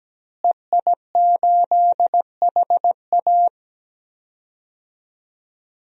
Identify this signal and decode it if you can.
Morse code, heard as a keyed tone, about 17 wpm, 700 Hz. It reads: EI8HA
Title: DXCC-100 Call Signs - 5 Characters - Repeated 3x - 17wpm